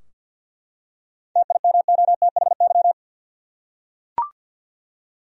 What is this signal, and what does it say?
Morse code, heard as a keyed tone, about 51 wpm, 700 Hz. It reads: TIMOTHY
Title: USA Common Names - 50wpm